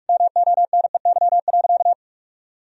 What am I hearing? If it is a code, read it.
Morse code, heard as a keyed tone, about 46 wpm, 700 Hz. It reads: MONEY.
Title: Sentences from Top 1000 Words - Review Part 3 - 45wpm